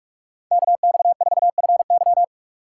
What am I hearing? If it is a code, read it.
Morse code, heard as a keyed tone, about 43 wpm, 700 Hz. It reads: KX4FY
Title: USA Call Signs - 8x Character Spacing - 45wpm